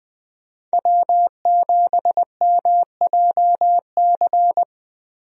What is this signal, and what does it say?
Morse code, heard as a keyed tone, about 20 wpm, 700 Hz. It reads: W7MJC
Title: USA Call Signs - 5 Characters - 20wpm